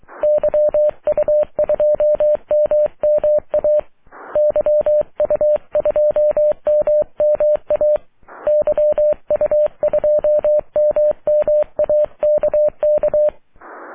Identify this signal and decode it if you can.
Morse code, heard as a keyed tone, about 23 wpm, 600 Hz. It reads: YU2MMA YU2MMA YU2MMA KK
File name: wp / la8aja yu2mma aurora 05 aug 2011